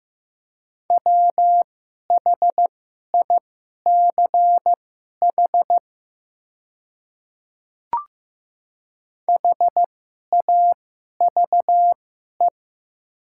Morse code, heard as a keyed tone, about 15 wpm, spaced as for 11 wpm, 700 Hz. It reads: WHICH HAVE
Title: 2x Character Spacing - Top 100 Words - 15wpm